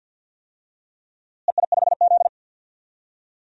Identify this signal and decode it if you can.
Morse code, heard as a keyed tone, about 50 wpm, 700 Hz. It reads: EI5Z
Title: DXCC-100 Call Signs - 4 Characters - Repeated 3x - 50wpm